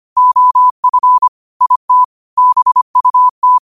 Morse code, heard as a keyed tone, about 25 wpm, 1000 Hz. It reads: OF IT BUT